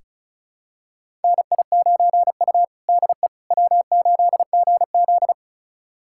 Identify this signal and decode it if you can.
Morse code, heard as a keyed tone, about 34 wpm, 700 Hz. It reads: NI9U DE W8GZ
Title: USA Callsign DE Callsign - 4 Characters - 35wpm